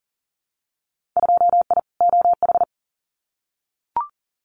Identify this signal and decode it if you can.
Morse code, heard as a keyed tone, about 40 wpm, 700 Hz. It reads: 2I OH
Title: Winter Field Day Exchanges 40wpm